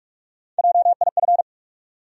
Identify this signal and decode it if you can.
Morse code, heard as a keyed tone, about 46 wpm, 700 Hz. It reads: JIF
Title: USA Call Sign Postfixes - 3 Letters 45wpm